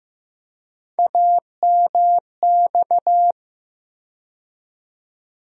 Morse code, heard as a keyed tone, about 15 wpm, 700 Hz. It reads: AMX
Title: USA Call Sign Postfixes - 3 Letters 15wpm